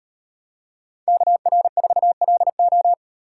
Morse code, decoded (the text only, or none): KR4LO